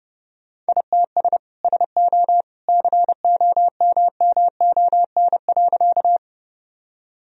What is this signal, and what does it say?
Morse code, heard as a keyed tone, about 30 wpm, 700 Hz. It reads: ITS SO COMMON.
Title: Sentences from Top 1000 Words - Review Part 3 - 30wpm